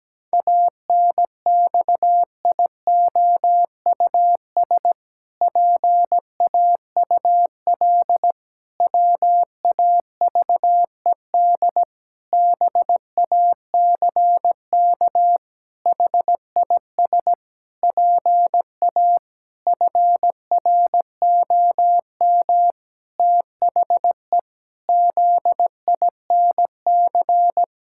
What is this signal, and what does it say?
Morse code, heard as a keyed tone, about 17 wpm, 700 Hz. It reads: ANXIOUS PAUL WAVED BACK HIS PA FROM THE ZINC
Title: Pangram Challenge v2 - 17wpm